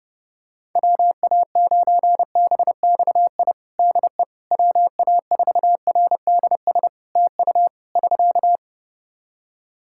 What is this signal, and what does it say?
Morse code, heard as a keyed tone, about 32 wpm, 700 Hz. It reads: WA9BXI DE WA4RDS TU <SK>
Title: Practice QSO - Exchange 4 - Ending - 30wpm